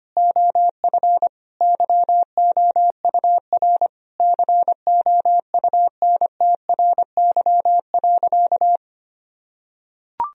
Morse code, heard as a keyed tone, about 25 wpm, 700 Hz. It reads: OF YOUR COUNTRY.
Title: Sentences from Top 1000 Words - Review Part 3 - 25wpm